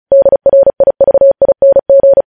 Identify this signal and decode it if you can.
Morse code, heard as a keyed tone, about 36 wpm, 550 Hz. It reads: DRIVING
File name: podcast / News headlines-1x-35x35